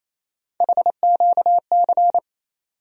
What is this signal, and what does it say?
Morse code, heard as a keyed tone, about 28 wpm, 700 Hz. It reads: HQC
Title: USA Call Sign Postfixes - 3 Letters 28wpm